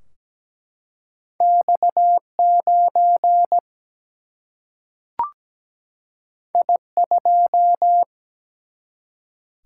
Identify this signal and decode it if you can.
Morse code, heard as a keyed tone, about 17 wpm, 700 Hz. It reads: X 9 I 2
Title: Single-Letter Single-Number 17wpm